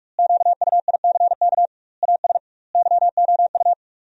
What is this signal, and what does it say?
Morse code, heard as a keyed tone, about 46 wpm, 700 Hz. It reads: QUICK AS YOU
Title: 2x Word-Spacing - Sentences from Top 500 Words - 45wpm